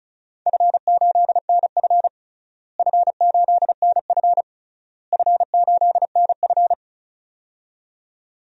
Morse code, heard as a keyed tone, about 36 wpm, 700 Hz. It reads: F8NF F8NF F8NF
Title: DXCC-100 Call Signs - 4 Characters - Repeated 3x - 35wpm